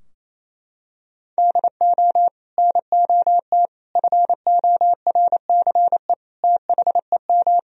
Morse code, heard as a keyed tone, about 28 wpm, 700 Hz. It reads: DO NOT FORCE THEM
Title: Sentences from Top 1000 Words - Review Part 3 - 28wpm